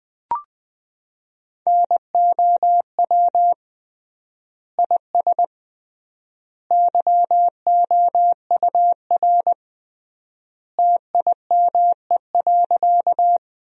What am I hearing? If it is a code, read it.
Morse code, heard as a keyed tone, about 20 wpm, 700 Hz. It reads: NOW IS YOUR TIME.
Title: 3x Word-Spacing - Sentences from Top 100 Words - 20wpm